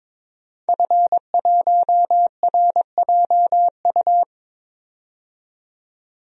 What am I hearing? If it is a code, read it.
Morse code, heard as a keyed tone, about 22 wpm, 700 Hz. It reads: F1RJU
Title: DXCC-100 Call Signs - 5 Characters - 22wpm